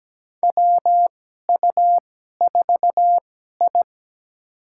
Morse code, heard as a keyed tone, about 17 wpm, spaced as for 12 wpm, 700 Hz. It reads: WU4I
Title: USA Call Signs - 4 Characters - 2x Character Spacing - 17wpm